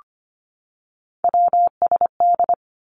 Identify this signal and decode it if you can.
Morse code, heard as a keyed tone, about 25 wpm, 700 Hz. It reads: WSD
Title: Three Challenging Letter-Number Combinations 25wpm